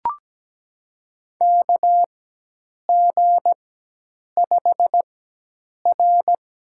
Morse code, heard as a keyed tone, about 17 wpm, spaced as for 8 wpm, 700 Hz. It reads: KG5R
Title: USA Call Signs - 6 Characters - 4x Character Spacing - 17wpm